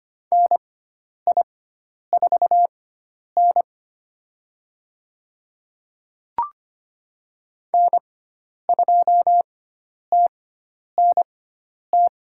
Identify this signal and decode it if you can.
Morse code, heard as a keyed tone, about 25 wpm, spaced as for 10 wpm, 700 Hz. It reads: NI4N N2TNT
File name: Lesson 16g - Letter c - Call Signs 25wpm